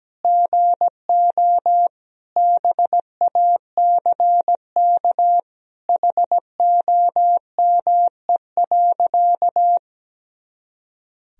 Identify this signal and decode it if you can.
Morse code, heard as a keyed tone, about 17 wpm, 700 Hz. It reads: GO BACK HOME.
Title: Sentences from Top 500 Words and Abbreviations 17wpm